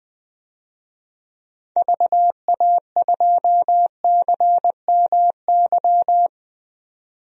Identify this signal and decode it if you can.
Morse code, heard as a keyed tone, about 21 wpm, 700 Hz. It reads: VA2CMY
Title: DXCC-100 Call Signs - 6 Characters - 20wpm